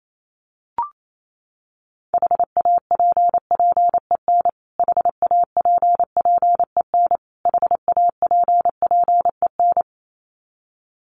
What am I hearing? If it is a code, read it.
Morse code, heard as a keyed tone, about 28 wpm, 700 Hz. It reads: HAPPEN HAPPEN HAPPEN
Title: Top 500 Words Repeated 28wpm